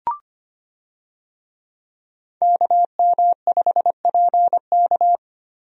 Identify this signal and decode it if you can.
Morse code, heard as a keyed tone, about 25 wpm, 700 Hz. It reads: KM5PK
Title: USA Call Signs - 5 Characters - Repeated 3x - 25wpm